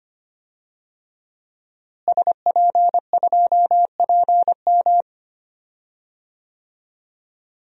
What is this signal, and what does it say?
Morse code, heard as a keyed tone, about 25 wpm, 700 Hz. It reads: SP2PM
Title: DXCC-100 Call Signs - 5 Characters - Repeated 3x - 25wpm